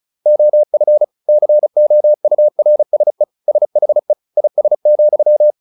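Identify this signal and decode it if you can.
Morse code, heard as a keyed tone, about 34 wpm, 600 Hz. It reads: OF COURSE SHE IS,
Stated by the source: Morse practice at 35 wpm from CW med Gnister, sending Ken Follett's novel Never in English.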